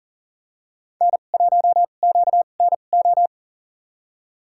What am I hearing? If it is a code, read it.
Morse code, heard as a keyed tone, about 42 wpm, spaced as for 30 wpm, 690 Hz. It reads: N1QNO
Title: USA Call Signs - 2x Character Spacing - 40wpm